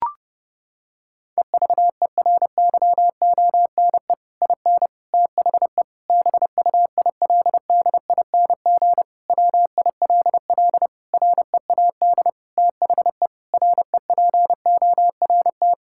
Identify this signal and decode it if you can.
Morse code, heard as a keyed tone, about 30 wpm, 700 Hz. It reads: EVERYONE IN THE BUILDING WILL READ THE REPORT
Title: Sentences from Top 1000 Words - Review Part 6 - 30wpm